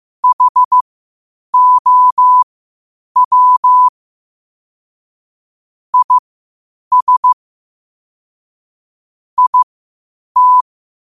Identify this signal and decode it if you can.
Morse code, heard as a keyed tone, about 15 wpm, spaced as for 8 wpm, 1000 Hz. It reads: HOW IS IT
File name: qotdfiles / 8wpm